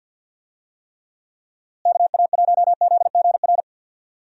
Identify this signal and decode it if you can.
Morse code, heard as a keyed tone, about 52 wpm, 700 Hz. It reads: KA1ZGR
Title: USA Call Signs - 6 Characters - Repeated 3x - 50wpm